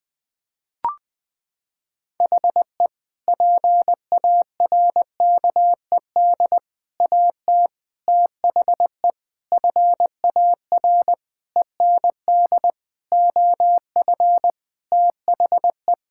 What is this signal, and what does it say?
Morse code, heard as a keyed tone, about 20 wpm, 700 Hz. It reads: HE PARKED AT THE FAR END OF THE
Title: Sentences with 5-Word Phrases - 20wpm